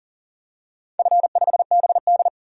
Morse code, heard as a keyed tone, about 40 wpm, 700 Hz. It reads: F5BD
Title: DXCC-100 Call Signs - 4 Characters - 40wpm